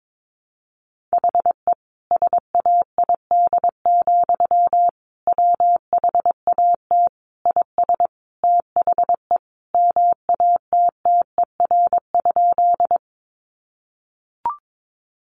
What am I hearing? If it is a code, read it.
Morse code, heard as a keyed tone, about 22 wpm, 700 Hz. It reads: HE SAID, WHAT IS THE MATTER?
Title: Sentences from Top 700 Words 22wpm